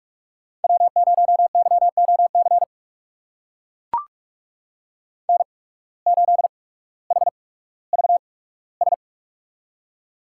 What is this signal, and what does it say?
Morse code, heard as a keyed tone, about 44 wpm, spaced as for 19 wpm, 690 Hz. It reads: W0YOC N8HVS
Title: USA Call Signs - 5 Characters - 8x Character Spacing - 45wpm